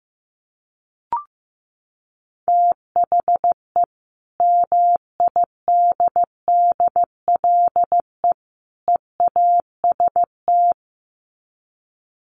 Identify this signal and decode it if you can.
Morse code, heard as a keyed tone, about 15 wpm, 700 Hz. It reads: THE MIDDLE EAST
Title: Sets of 3 Words 15wpm